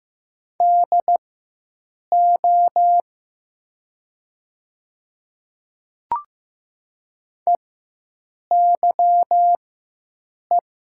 Morse code, heard as a keyed tone, about 15 wpm, spaced as for 7 wpm, 700 Hz. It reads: DO EYE
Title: 4x Character Spacing - Top 500 Words - 15wpm